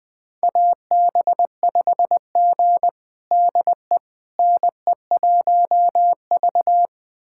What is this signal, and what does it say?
Morse code, heard as a keyed tone, about 20 wpm, 700 Hz. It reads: AB5G DE NE1V
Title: USA Callsign DE Callsign - 4 Characters - 20wpm